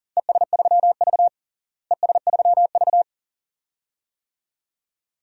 Morse code, heard as a keyed tone, about 40 wpm, 700 Hz. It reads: ES3V ES3V
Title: DXCC-100 Call Signs - 4 Characters - Repeated 2x - 40wpm